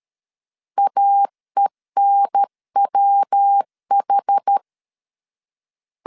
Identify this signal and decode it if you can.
Morse code, heard as a keyed tone, about 13 wpm, 800 Hz. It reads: AENWH
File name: CW / DET / 1 3 1500C S A 12wpm 2905